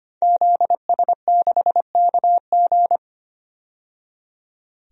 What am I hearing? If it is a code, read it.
Morse code, heard as a keyed tone, about 25 wpm, 700 Hz. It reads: ZS6KG
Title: DXCC-100 Call Signs - 5 Characters - 25wpm